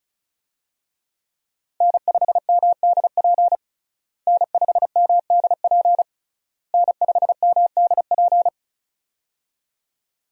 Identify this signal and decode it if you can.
Morse code, heard as a keyed tone, about 36 wpm, 700 Hz. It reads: N5MDP N5MDP N5MDP
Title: USA Call Signs - 5 Characters - Repeated 3x - 35wpm